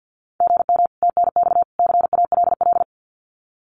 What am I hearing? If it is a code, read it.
Morse code, heard as a keyed tone, about 48 wpm, 700 Hz. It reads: GM TNX CALL
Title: Common QSO Phrases 50wpm